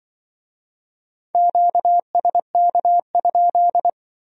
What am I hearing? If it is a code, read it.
Morse code, heard as a keyed tone, about 25 wpm, 700 Hz. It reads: QSK?
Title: Speed-Racing Q-Codes - 20wpm